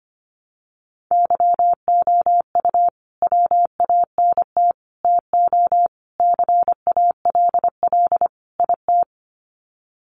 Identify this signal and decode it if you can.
Morse code, heard as a keyed tone, about 25 wpm, 700 Hz. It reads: YOU WANT TO CALL IT
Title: Sets of 5 Words Encore2 25wpm